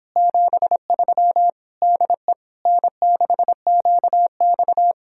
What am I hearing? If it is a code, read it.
Morse code, heard as a keyed tone, about 26 wpm, 700 Hz. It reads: 73 DE N6QX